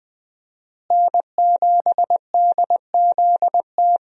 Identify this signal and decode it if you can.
Morse code, heard as a keyed tone, about 20 wpm, 700 Hz. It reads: N7DZT
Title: USA Call Signs - 5 Characters - Repeated 2x - 20wpm